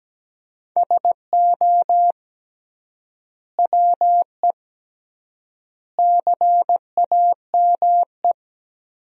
Morse code, heard as a keyed tone, about 17 wpm, 700 Hz. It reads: SO WE CAME
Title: 3x Word-Spacing - Sentences from Top 400 Words - 17wpm